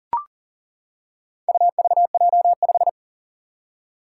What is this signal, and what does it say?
Morse code, heard as a keyed tone, about 37 wpm, 700 Hz. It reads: UVJ5
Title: Four Challenging Letter-Number Combinations 40wpm